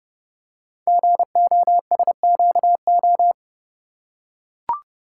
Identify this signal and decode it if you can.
Morse code, heard as a keyed tone, about 30 wpm, 700 Hz. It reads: GOSQO
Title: Code Groups - 5 Letters - 30wpm